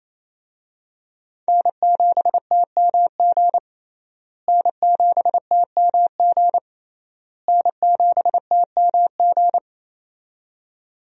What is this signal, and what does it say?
Morse code, heard as a keyed tone, about 27 wpm, 700 Hz. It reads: N7TMG N7TMG N7TMG
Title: USA Call Signs - 5 Characters - Repeated 3x - 28wpm